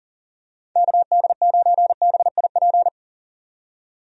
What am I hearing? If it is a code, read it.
Morse code, heard as a keyed tone, about 40 wpm, 700 Hz. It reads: KD9BIP
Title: USA Call Signs - 6 Characters - 40wpm